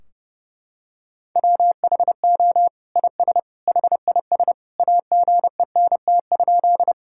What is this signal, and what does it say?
Morse code, heard as a keyed tone, about 30 wpm, 700 Hz. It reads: WHO IS HIS AGENT?
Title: Sentences from Top 1600 Words 30wpm